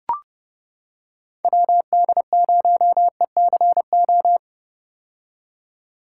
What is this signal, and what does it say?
Morse code, heard as a keyed tone, about 30 wpm, 700 Hz. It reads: WD0ECO
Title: K1USN SST QSO Practice - 30wpm